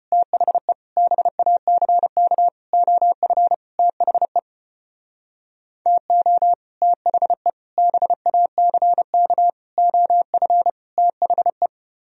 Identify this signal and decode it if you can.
Morse code, heard as a keyed tone, about 32 wpm, 700 Hz. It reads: THE BACK OF THE TO THE BACK OF THE